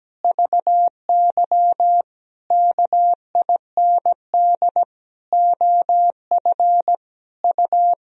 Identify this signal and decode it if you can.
Morse code, heard as a keyed tone, about 17 wpm, 700 Hz. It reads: VY KIND OF U
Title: Sentences from Top 300 Words and Abbreviations 17wpm